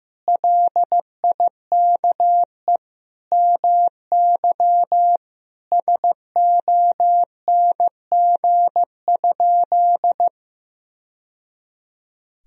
Morse code, heard as a keyed tone, about 15 wpm, 700 Hz. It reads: LIKE MY SONG?
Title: Sentences from Top 1000 Words - Review Part 2 - 15wpm